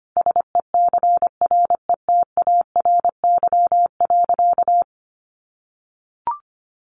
Morse code, heard as a keyed tone, about 25 wpm, 700 Hz. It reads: SECRETARY.
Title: Sentences from Top 1300 Words 25wpm